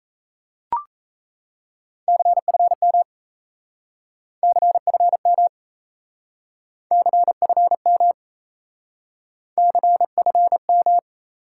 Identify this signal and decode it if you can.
Morse code, heard as a keyed tone, about 37 wpm, 700 Hz. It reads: CFM CFM CFM CFM